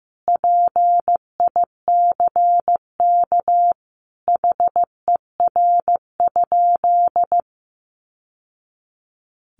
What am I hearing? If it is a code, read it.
Morse code, heard as a keyed tone, about 15 wpm, 700 Hz. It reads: PICK HER?
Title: Sentences from Top 1000 Words 15wpm